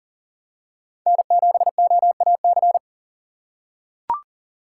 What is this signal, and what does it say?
Morse code, heard as a keyed tone, about 40 wpm, 700 Hz. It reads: N7OAC
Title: USA Call Signs - 5 Characters - 6x Character Spacing - 40wpm